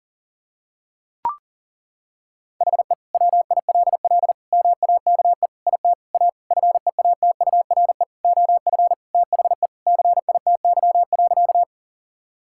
Morse code, heard as a keyed tone, about 37 wpm, 700 Hz. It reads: HE WILL MAKE IT A FEATURE OF THE CITY.
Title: Sentences from Top 1000 Words - Review Part 5 - 40wpm